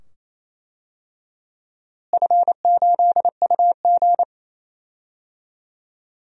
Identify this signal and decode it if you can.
Morse code, heard as a keyed tone, about 27 wpm, 700 Hz. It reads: F8UG
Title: DXCC-100 Call Signs - 4 Characters - Repeated 2x - 28wpm